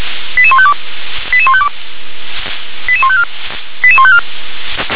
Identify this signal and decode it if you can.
Five-tone selective call (ZVEI), heard as quick runs of five tones, tones of about 70 ms. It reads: 88152 88152 88154 88154
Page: /357/SDR/ZVEI/88101/2021.10.12/